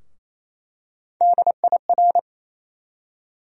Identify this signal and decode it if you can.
Morse code, heard as a keyed tone, about 28 wpm, 700 Hz. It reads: DIR
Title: USA Call Sign Postfixes - 3 Letters 28wpm